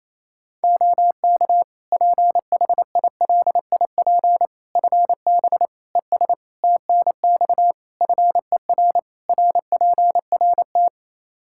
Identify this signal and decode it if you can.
Morse code, heard as a keyed tone, about 28 wpm, 700 Hz. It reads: OK PHILIP FB ES TNX FER RPRT
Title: K1USN SST QSO Practice 28wpm